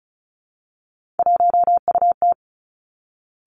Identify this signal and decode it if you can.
Morse code, heard as a keyed tone, about 36 wpm, 700 Hz. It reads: 1UT
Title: Three Challenging Letter-Number Combinations 35wpm